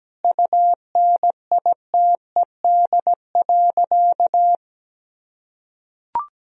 Morse code, heard as a keyed tone, about 17 wpm, 700 Hz. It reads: UNITED.
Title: Sentences from Top 1000 Words - Review Part 4 - 17wpm